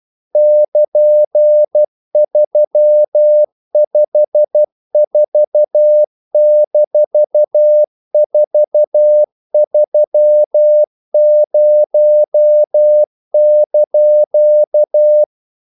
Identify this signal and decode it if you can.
Morse code, heard as a keyed tone, about 12 wpm, 600 Hz. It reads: <KN>354-430)
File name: CW / Practice / EnglishQuotations0112